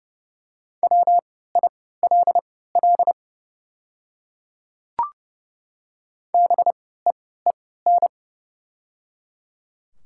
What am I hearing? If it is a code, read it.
Morse code, heard as a keyed tone, about 30 wpm, spaced as for 17 wpm, 700 Hz. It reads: WILL BEEN
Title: 3x Character Spacing - Top 100 Words - 30wpm